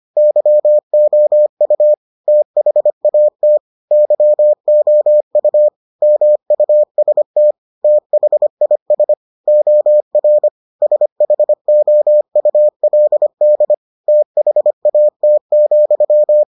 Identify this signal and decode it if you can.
Morse code, heard as a keyed tone, about 25 wpm, 600 Hz. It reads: YOU THAT YOU MUST THIS OR SHOULD THAT,